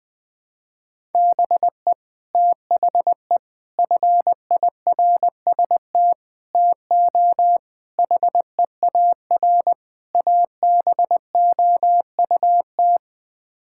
Morse code, heard as a keyed tone, about 20 wpm, 700 Hz. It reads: BE THE FIRST TO HEAR ABOUT
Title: Sets of 6 Words - 20wpm